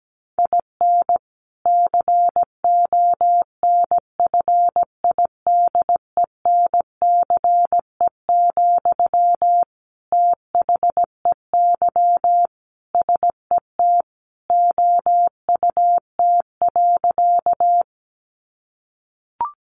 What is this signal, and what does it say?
Morse code, heard as a keyed tone, about 17 wpm, 700 Hz. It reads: IN CONFIDENCE, THEY SET OUT.
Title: Sentences from Top 1500 Words 17wpm